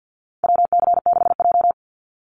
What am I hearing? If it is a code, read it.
Morse code, heard as a keyed tone, about 48 wpm, 700 Hz. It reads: WC6J
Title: NA SOTA Activator Callsigns 50wpm